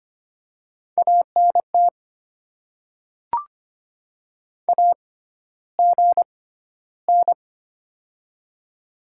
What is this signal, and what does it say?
Morse code, heard as a keyed tone, about 25 wpm, spaced as for 13 wpm, 700 Hz. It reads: ANT AGN
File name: Top 20 QSO Elements - 6x Character Spacing - 25wpm